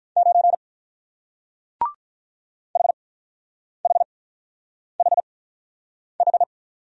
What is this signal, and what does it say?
Morse code, heard as a keyed tone, about 45 wpm, spaced as for 9 wpm, 700 Hz. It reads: 9 HHHH